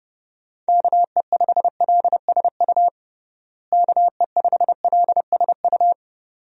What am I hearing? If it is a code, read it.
Morse code, heard as a keyed tone, about 30 wpm, 700 Hz. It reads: KE5LSU KE5LSU
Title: USA Call Signs - 6 Characters - Repeated 2x - 30wpm